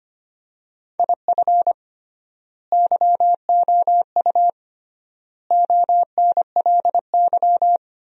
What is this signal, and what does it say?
Morse code, heard as a keyed tone, about 25 wpm, 700 Hz. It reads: IF YOU ONLY
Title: 3x Word-Spacing - Sentences from Top 400 Words - 25wpm